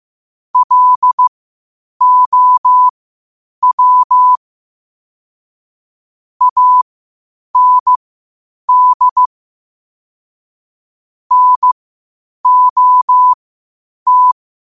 Morse code, heard as a keyed tone, about 15 wpm, spaced as for 8 wpm, 1000 Hz. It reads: LOW AND NOT
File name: qotdfiles / 8wpm